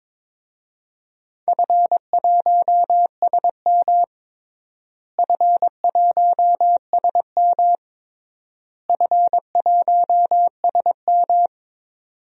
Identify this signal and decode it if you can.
Morse code, heard as a keyed tone, about 22 wpm, 700 Hz. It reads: F1SM F1SM F1SM
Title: DXCC-100 Call Signs - 4 Characters - Repeated 3x - 22wpm